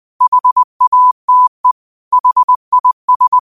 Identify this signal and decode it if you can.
Morse code, heard as a keyed tone, about 20 wpm, 1000 Hz. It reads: HATE HIS
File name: qotdfiles / 20wpm